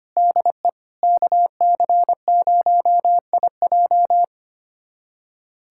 Morse code, heard as a keyed tone, about 25 wpm, 700 Hz. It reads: DE KC0IJ
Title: USA Callsign DE Callsign - 5 Characters - 25wpm